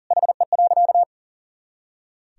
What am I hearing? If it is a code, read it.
Morse code, heard as a keyed tone, about 40 wpm, 700 Hz. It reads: HE.
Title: Sentences from Top 400 Words and Abbreviations 40wpm